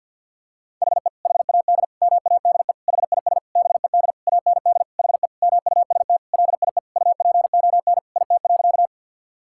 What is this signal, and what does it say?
Morse code, heard as a keyed tone, about 50 wpm, 700 Hz. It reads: HE HAD MADE HIS BED AND HE MUST LIE UPON IT.
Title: Sentences from Top 1600 Words 50wpm